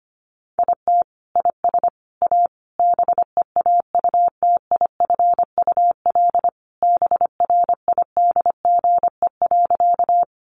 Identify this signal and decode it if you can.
Morse code, heard as a keyed tone, about 25 wpm, 700 Hz. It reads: IT IS A BEAUTIFUL BRIDGE.
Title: Sentences from Top 1200 Words 25wpm